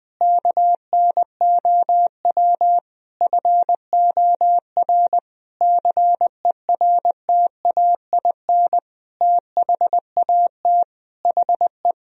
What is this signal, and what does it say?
Morse code, heard as a keyed tone, about 20 wpm, 700 Hz. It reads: KNOW FOR CERTAIN THAT HE
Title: Sentences from Top 1000 Words - Review Part 2 - 20wpm